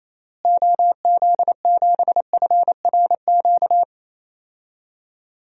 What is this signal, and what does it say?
Morse code, heard as a keyed tone, about 28 wpm, 700 Hz. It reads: OZ7FRQ
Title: DXCC-100 Call Signs - 6 Characters - Repeated 3x - 28wpm